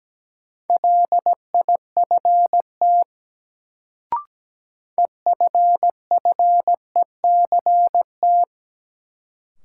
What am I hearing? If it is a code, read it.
Morse code, heard as a keyed tone, about 17 wpm, 700 Hz. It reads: LIFT EFFECT
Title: Sentences from Top 1000 Words - New Word Drill - 17wpm